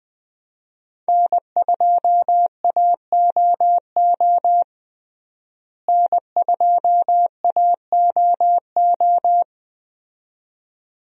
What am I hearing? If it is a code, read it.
Morse code, heard as a keyed tone, about 20 wpm, 700 Hz. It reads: N2AOO N2AOO
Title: USA Call Signs - 5 Characters - Repeated 2x - 20wpm